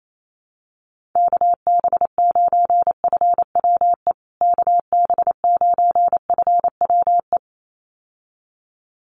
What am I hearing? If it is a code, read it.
Morse code, heard as a keyed tone, about 28 wpm, 700 Hz. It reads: KB9FWE KB9FWE
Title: USA Call Signs Repeated 2x 28wpm